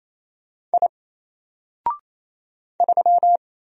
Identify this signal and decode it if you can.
Morse code, heard as a keyed tone, about 27 wpm, 700 Hz. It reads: I 3
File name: Single-Letter-Number-28wpm